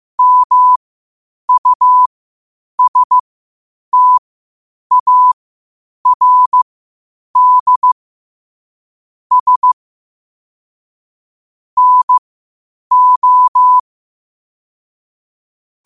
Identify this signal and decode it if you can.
Morse code, heard as a keyed tone, about 15 wpm, spaced as for 8 wpm, 1000 Hz. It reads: MUSTARDS NO
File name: qotdfiles / 8wpm